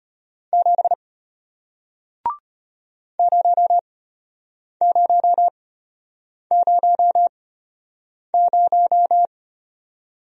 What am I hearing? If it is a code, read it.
Morse code, heard as a keyed tone, about 33 wpm, spaced as for 8 wpm, 700 Hz. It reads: 7 0000